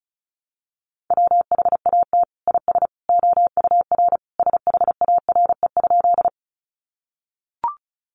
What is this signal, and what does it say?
Morse code, heard as a keyed tone, about 34 wpm, 700 Hz. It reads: WHAT IS OUR SHARE?
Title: Sentences from Top 900 Words 35wpm